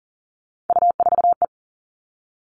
Morse code, heard as a keyed tone, about 40 wpm, 700 Hz. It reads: U4E
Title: All Cut Numbers - 3-Digit 40wpm